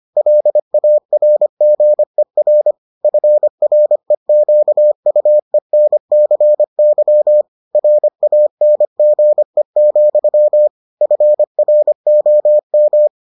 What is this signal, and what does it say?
Morse code, heard as a keyed tone, about 25 wpm, 600 Hz. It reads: LARGER FREQUENCY RANGE, FROM